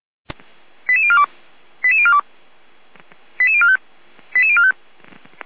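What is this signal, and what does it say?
Five-tone selective call (ZVEI), heard as quick runs of five tones, tones of about 70 ms. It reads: 88042 88042 88045 88045
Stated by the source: Five-tone ZVEI selective call received by SDR.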